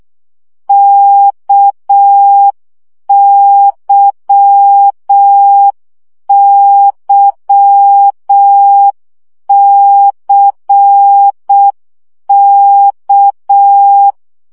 Morse code, heard as a keyed tone, about 6 wpm, 800 Hz. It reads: KYYCK
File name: CW gyakorlas k c y 5 baud